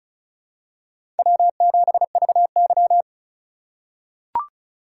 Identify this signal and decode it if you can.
Morse code, heard as a keyed tone, about 36 wpm, 700 Hz. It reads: W7VY
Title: USA Call Signs - 4 Characters - 4x Character Spacing - 35wpm